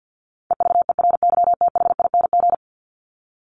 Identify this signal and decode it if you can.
Morse code, heard as a keyed tone, about 50 wpm, 700 Hz. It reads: EVERYTHING
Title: Top 501-750 Words 50wpm